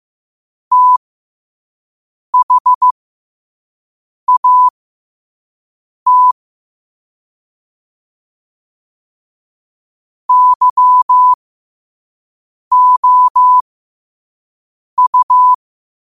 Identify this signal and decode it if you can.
Morse code, heard as a keyed tone, about 15 wpm, spaced as for 5 wpm, 1000 Hz. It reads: THAT YOU
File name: qotdfiles / 5wpm